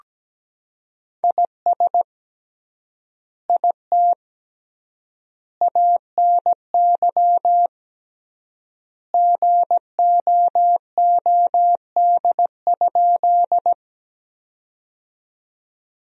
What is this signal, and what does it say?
Morse code, heard as a keyed tone, about 17 wpm, 700 Hz. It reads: IS IT ANY GOOD?
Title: 3x Word-Spacing - Sentences from Top 200 Words - 17wpm